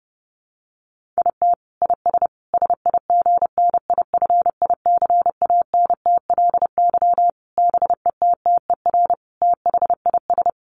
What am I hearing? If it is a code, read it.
Morse code, heard as a keyed tone, about 30 wpm, 700 Hz. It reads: IT IS SIGNIFICANTLY BETTER THIS
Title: Sentences from Top 1500 Words 30wpm